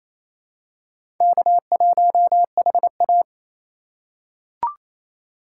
Morse code, heard as a keyed tone, about 28 wpm, 700 Hz. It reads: K1HA
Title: USA Call Signs - 4 Characters - 3x Character Spacing - 28wpm